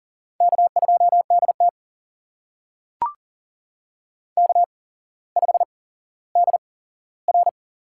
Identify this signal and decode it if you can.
Morse code, heard as a keyed tone, about 40 wpm, spaced as for 17 wpm, 700 Hz. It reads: K2DT K5DR
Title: USA Call Signs - 4 Characters - 8x Character Spacing - 40wpm